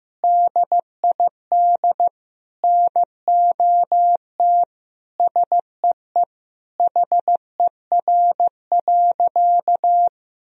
Morse code, heard as a keyed tone, about 15 wpm, 700 Hz. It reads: DID NOT SEE HER.
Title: Sentences from Top 1000 Words - Review Part 4 - 15wpm